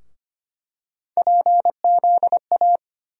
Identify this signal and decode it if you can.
Morse code, heard as a keyed tone, about 25 wpm, 700 Hz. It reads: PZA
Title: USA Call Sign Postfixes - 3 Letters 25wpm